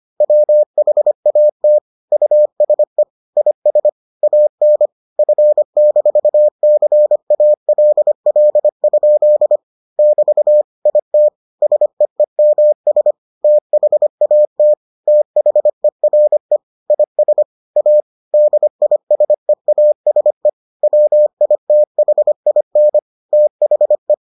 Morse code, heard as a keyed tone, about 25 wpm, 600 Hz. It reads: WHAT USE IS AN F-CALL? <BT> IT SEEMS THAT THERE IS A DISEASE WITHIN THE